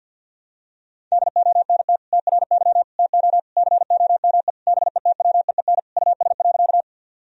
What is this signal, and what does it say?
Morse code, heard as a keyed tone, about 50 wpm, 700 Hz. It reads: DONT TRY TO COME BETWEEN US.
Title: Sentences from Top 1000 Words - Review Part 1 - 50wpm